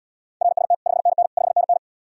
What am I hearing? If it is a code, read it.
Morse code, heard as a keyed tone, about 79 wpm, 700 Hz. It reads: RST 5NN 5NN